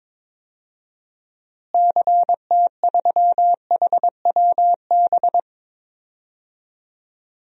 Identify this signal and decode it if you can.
Morse code, heard as a keyed tone, about 22 wpm, 700 Hz. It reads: CT3HWB
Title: DXCC-100 Call Signs - 6 Characters - Repeated 3x - 22wpm